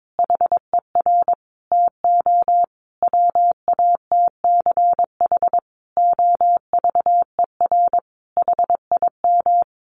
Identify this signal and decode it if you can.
Morse code, heard as a keyed tone, about 22 wpm, 700 Hz. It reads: HER TO WATCH OVER HIM
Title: Sentences from Top 1500 Words 22wpm